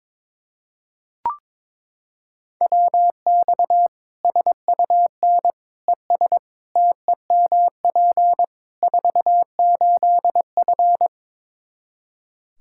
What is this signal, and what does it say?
Morse code, heard as a keyed tone, about 22 wpm, 700 Hz. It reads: WX SUN ES TEMP 48F
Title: K1USN SST QSO Practice 22wpm